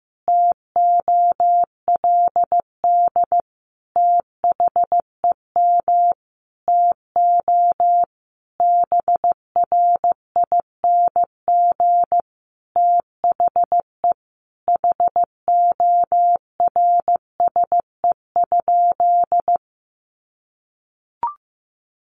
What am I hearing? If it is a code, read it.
Morse code, heard as a keyed tone, about 15 wpm, 700 Hz. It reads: TOLD THEM TO BRING THE HORSE?
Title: Sentences from Top 500 Words 15wpm